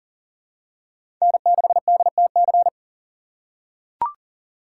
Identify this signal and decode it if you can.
Morse code, heard as a keyed tone, about 40 wpm, 700 Hz. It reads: N6DTC
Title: USA Call Signs - 5 Characters - 6x Character Spacing - 40wpm